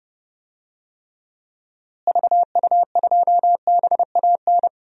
Morse code, heard as a keyed tone, about 30 wpm, 700 Hz. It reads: VU2BAN
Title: DXCC-100 Call Signs - 6 Characters - Repeated 3x - 30wpm